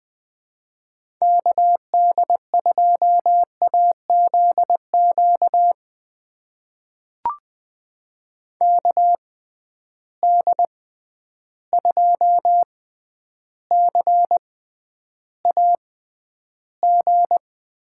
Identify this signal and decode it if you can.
Morse code, heard as a keyed tone, about 20 wpm, spaced as for 10 wpm, 700 Hz. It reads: KD2AZQ KD2CAG
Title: USA Call Signs - 6 Characters - 6x Character Spacing - 20wpm